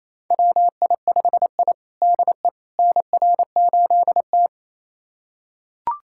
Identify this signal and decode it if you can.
Morse code, heard as a keyed tone, about 28 wpm, 700 Hz. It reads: WI5I DE NR8T
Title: USA Callsign DE Callsign - 4 Characters - 28wpm